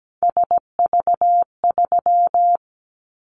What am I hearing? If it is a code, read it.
Morse code, heard as a keyed tone, about 17 wpm, 700 Hz. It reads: SV3
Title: Callsign Prefixes for Top 50 DXCC - 17wpm